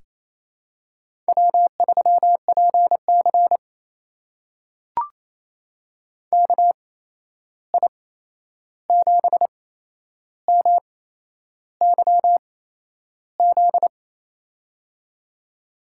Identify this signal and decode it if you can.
Morse code, heard as a keyed tone, about 28 wpm, spaced as for 11 wpm, 700 Hz. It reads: W3PC KI7MYZ
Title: USA Call Signs - 8x Character Spacing - 28wpm